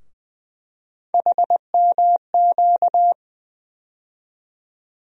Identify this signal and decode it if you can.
Morse code, heard as a keyed tone, about 20 wpm, 700 Hz. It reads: HMQ
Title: USA Call Sign Postfixes - 3 Letters 20wpm